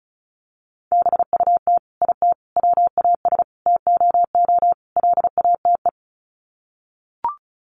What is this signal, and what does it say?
Morse code, heard as a keyed tone, about 34 wpm, 700 Hz. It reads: BUT IT WAS TOO LATE
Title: Sets of 5 Words Encore2 35wpm